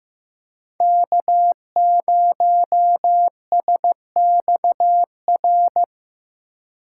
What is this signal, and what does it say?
Morse code, heard as a keyed tone, about 15 wpm, 700 Hz. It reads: K0SXR
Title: USA Call Signs - 4x Character Spacing - 15wpm